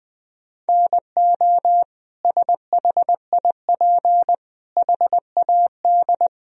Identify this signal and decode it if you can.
Morse code, heard as a keyed tone, about 20 wpm, 700 Hz. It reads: NO SHIP HAD
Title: Sentences from Top 1000 Words - Review Part 4 - 20wpm